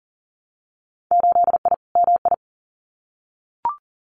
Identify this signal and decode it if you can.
Morse code, heard as a keyed tone, about 43 wpm, 700 Hz. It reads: 8I MI
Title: Winter Field Day Exchanges Encore 40wpm